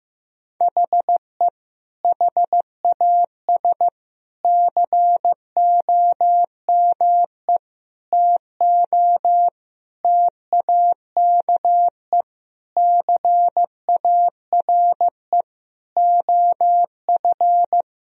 Morse code, heard as a keyed tone, about 15 wpm, 700 Hz. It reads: HE HAS COME TO TAKE CARE OF
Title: Sentences from Top 1000 Words - Review Part 1 - 15wpm